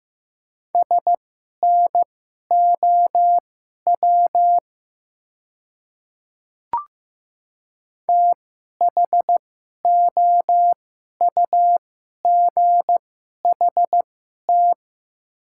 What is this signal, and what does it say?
Morse code, heard as a keyed tone, about 15 wpm, spaced as for 11 wpm, 700 Hz. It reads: SNOW THOUGHT
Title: 2x Character Spacing - Top 500 Words - 15wpm